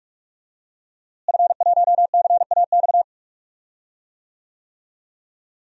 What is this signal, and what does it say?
Morse code, heard as a keyed tone, about 45 wpm, 700 Hz. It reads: F1CAX
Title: DXCC-100 Call Signs - 5 Characters - Repeated 2x - 45wpm